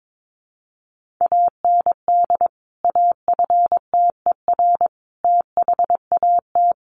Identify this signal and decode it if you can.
Morse code, heard as a keyed tone, about 22 wpm, 700 Hz. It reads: AND AFTER THAT
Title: Sentences from Top 1400 Words 22wpm